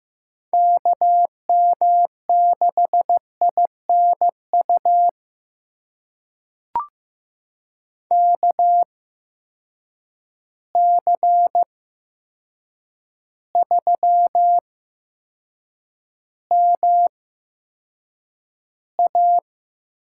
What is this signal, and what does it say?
Morse code, heard as a keyed tone, about 15 wpm, spaced as for 7 wpm, 700 Hz. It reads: KM6INU KC3MA
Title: USA Call Signs - 6 Characters - 8x Character Spacing - 15wpm